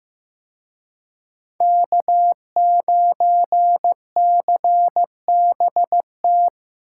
Morse code, heard as a keyed tone, about 18 wpm, 700 Hz. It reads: K9CBT
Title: Speed Racing - USA 1x3 Call Signs 15wpm